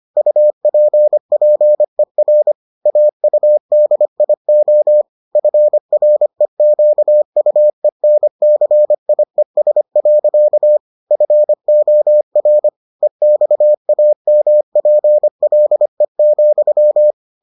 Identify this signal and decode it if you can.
Morse code, heard as a keyed tone, about 25 wpm, 600 Hz. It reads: UPPER AUDIO FREQUENCIES. FOR EXAMPLE,